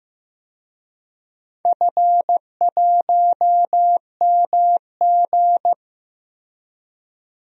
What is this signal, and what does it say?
Morse code, heard as a keyed tone, about 15 wpm, 700 Hz. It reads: F1MG
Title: DXCC-100 Call Signs - 4 Characters - Repeated 3x - 15wpm